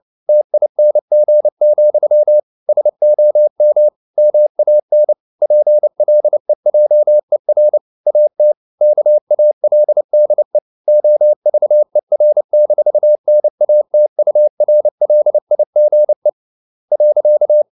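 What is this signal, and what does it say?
Morse code, heard as a keyed tone, about 29 wpm, 600 Hz. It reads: TING, SOM MAN PLEJER AT KALDE OVER-NATURLIGE.
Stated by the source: Fast (QRQ) Morse at 29 wpm, from CW med Gnister.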